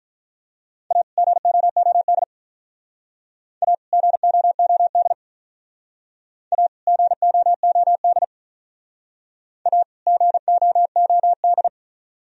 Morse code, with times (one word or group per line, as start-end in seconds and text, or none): 0.9-1.0 A
1.2-2.2 GOOD
3.6-3.7 A
3.9-5.1 GOOD
6.5-6.7 A
6.9-8.2 GOOD
9.7-9.8 A
10.1-11.7 GOOD